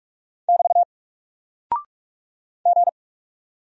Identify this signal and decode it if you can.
Morse code, heard as a keyed tone, about 44 wpm, 700 Hz. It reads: <BT> G
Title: Single Character (Letter:Number:Punctuation:Prosign) Alt 45wpm